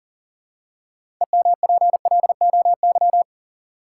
Morse code, heard as a keyed tone, about 40 wpm, 700 Hz. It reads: EMPLOY
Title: Six Letter Words - 40wpm